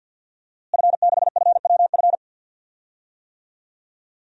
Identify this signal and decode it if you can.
Morse code, heard as a keyed tone, about 50 wpm, 700 Hz. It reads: F6FWF
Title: DXCC-100 Call Signs - 5 Characters - Repeated 2x - 50wpm